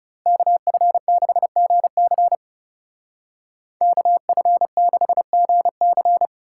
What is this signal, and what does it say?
Morse code, heard as a keyed tone, about 39 wpm, 700 Hz. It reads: KF6GC KF6GC